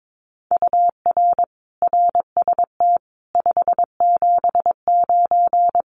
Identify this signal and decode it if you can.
Morse code, heard as a keyed tone, about 22 wpm, 700 Hz. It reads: UR RST 579
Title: Common QSO Phrases Encore 22wpm